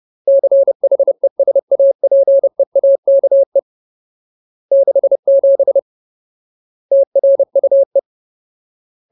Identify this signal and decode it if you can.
Morse code, heard as a keyed tone, about 30 wpm, 550 Hz. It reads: CHESAPEAKE 67 TRUE
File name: Ultimate Word List / 30WPM Ultimate